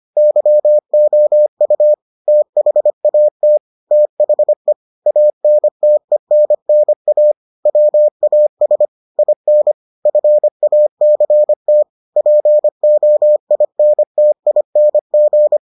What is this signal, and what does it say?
Morse code, heard as a keyed tone, about 25 wpm, 600 Hz. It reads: YOU THAT THE ANTENNA WAS IN FACT POINTING